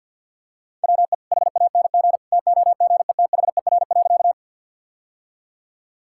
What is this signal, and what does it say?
Morse code, heard as a keyed tone, about 47 wpm, 700 Hz. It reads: WE HANG TOGETHER.
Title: Sentences from Top 1600 Words 50wpm